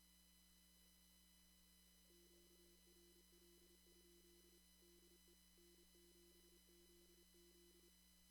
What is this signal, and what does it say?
Morse code, heard as a keyed tone, about 27 wpm, 340 Hz. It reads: QRL? DE N7ZZ
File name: CW / kt5j wpx cw-op-k5pi